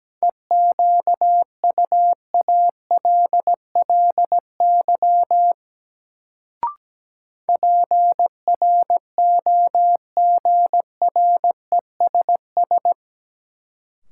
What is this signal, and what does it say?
Morse code, heard as a keyed tone, about 17 wpm, 700 Hz. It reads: EQUALLY PROGRESS
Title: Sentences from Top 1400 Words - New Word Drill - 17wpm